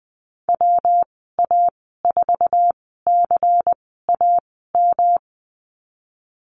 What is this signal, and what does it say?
Morse code, heard as a keyed tone, about 20 wpm, spaced as for 15 wpm, 700 Hz. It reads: WA4CAM
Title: 173 Lesson 28g - Number 9 - Call Signs 20wpm